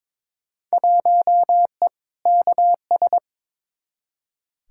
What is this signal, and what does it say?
Morse code, heard as a keyed tone, about 22 wpm, 700 Hz. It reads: 1E KS
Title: Field Day Exchanges - 22wpm